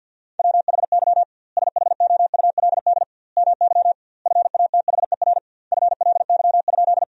Morse code, heard as a keyed tone, about 50 wpm, 700 Hz. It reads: WHY SHOULD MY FATHER FLY?
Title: 2x Word-Spacing - Sentences from Top 500 Words - 50wpm